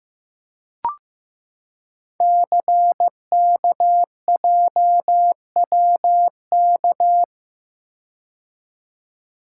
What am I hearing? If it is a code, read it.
Morse code, heard as a keyed tone, about 15 wpm, 700 Hz. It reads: CKJWK
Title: Code Groups - 5 Letters - 15wpm